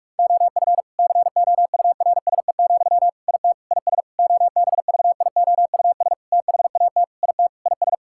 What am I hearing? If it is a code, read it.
Morse code, heard as a keyed tone, about 46 wpm, 700 Hz. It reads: OF COURSE, IT IS OBVIOUS THAT IT IS
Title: Sentences from Top 1600 Words 45wpm